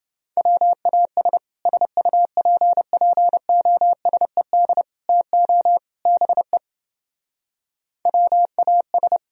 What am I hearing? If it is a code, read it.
Morse code, heard as a keyed tone, about 29 wpm, 700 Hz. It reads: WAS SUPPOSED TO BE WAS